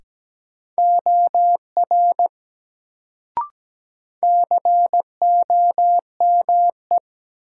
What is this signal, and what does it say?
Morse code, heard as a keyed tone, about 17 wpm, 700 Hz. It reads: OR COME
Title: Sentences from Top 100 Words - New Word Drill - 17wpm